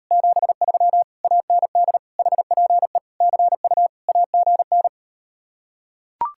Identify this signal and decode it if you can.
Morse code, heard as a keyed tone, about 38 wpm, 700 Hz. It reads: 73 AND HPE CU AGN